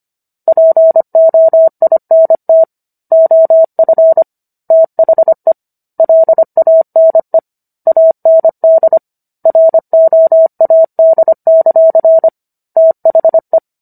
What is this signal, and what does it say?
Morse code, heard as a keyed tone, about 25 wpm, 650 Hz. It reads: POINT OF THE LANE AND ROAD; THE